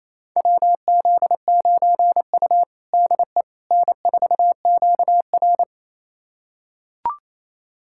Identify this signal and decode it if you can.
Morse code, heard as a keyed tone, about 28 wpm, 700 Hz. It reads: WZ9U DE N4QR
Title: USA Callsign DE Callsign - 4 Characters - 28wpm